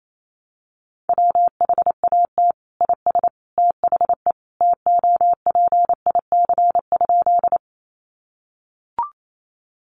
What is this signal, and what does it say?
Morse code, heard as a keyed tone, about 28 wpm, 700 Hz. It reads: WHAT IS THE TOPIC?
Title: Sentences from Top 1200 Words 28wpm